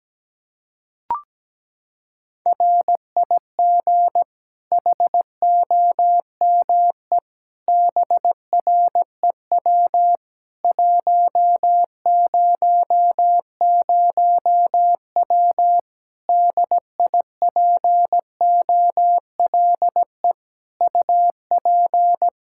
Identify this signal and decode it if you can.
Morse code, heard as a keyed tone, about 17 wpm, 700 Hz. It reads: RIG HOME BREW 100W DIPOLE UP
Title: Practice QSO - Exchange 2 - Info - 17wpm